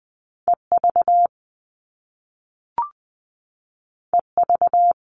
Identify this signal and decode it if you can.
Morse code, heard as a keyed tone, about 20 wpm, 700 Hz. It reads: E V E V
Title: All Cut Numbers - 2-Digit 20wpm